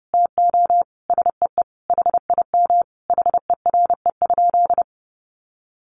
Morse code, heard as a keyed tone, about 30 wpm, 700 Hz. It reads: TO SEE HIM HERE?
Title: Sentences from Top 1100 Words 30wpm